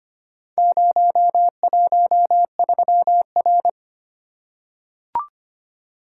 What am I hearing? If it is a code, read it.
Morse code, heard as a keyed tone, about 25 wpm, 700 Hz. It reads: 013R
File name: practice / 25wpm / Four Letter-Number Combinations - 25wpm